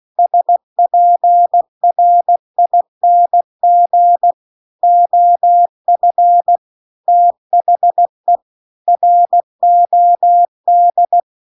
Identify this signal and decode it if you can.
Morse code, heard as a keyed tone, about 16 wpm, 700 Hz. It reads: SPRING OF THE ROD
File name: CW-The-Big-Twoheaded-River-2-16WPM